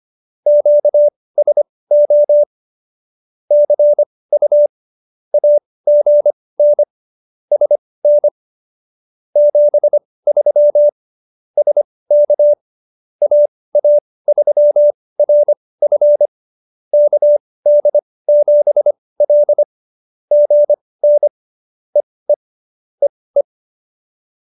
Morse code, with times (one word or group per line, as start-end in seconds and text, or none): 0.5-2.4 QSO
3.5-4.7 CU
5.3-6.8 AGN
7.5-8.3 SN
9.4-10.9 73
11.6-12.5 SK
13.2-16.3 AA3RF
16.9-19.6 KD7L
20.3-21.3 GN
22.0-22.3 EE
23.0-23.4 EE